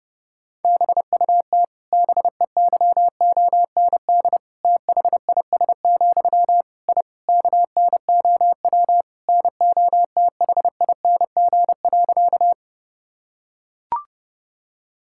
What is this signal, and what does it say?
Morse code, heard as a keyed tone, about 30 wpm, 700 Hz. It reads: BUT BEYOND THIS, I KNOW NOTHING.
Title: Sentences from Top 1000 Words - Review Part 6 - 30wpm